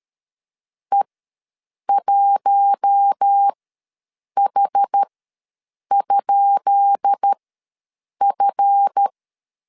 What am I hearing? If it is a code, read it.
Morse code, heard as a keyed tone, about 13 wpm, spaced as for 7 wpm, 800 Hz. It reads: E1H?F
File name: LEZIONI / 10 SIGNS / 10B INT ws 12wpm 3527